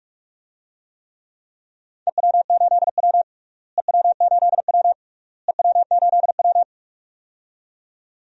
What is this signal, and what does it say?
Morse code, heard as a keyed tone, about 42 wpm, 700 Hz. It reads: EW8W EW8W EW8W
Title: DXCC-100 Call Signs - 4 Characters - Repeated 3x - 45wpm